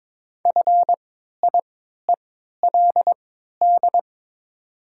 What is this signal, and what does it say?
Morse code, heard as a keyed tone, about 22 wpm, spaced as for 12 wpm, 700 Hz. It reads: FIELD
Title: 3x Character Spacing - Top 500 Words - 22wpm